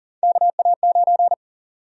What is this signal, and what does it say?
Morse code, heard as a keyed tone, about 40 wpm, 700 Hz. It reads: KA9
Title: USA Call Sign Prefixes - 40wpm